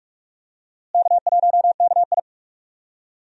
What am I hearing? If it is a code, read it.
Morse code, heard as a keyed tone, about 44 wpm, 700 Hz. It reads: K1KI
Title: USA Call Signs - 4 Characters - Repeated 2x - 45wpm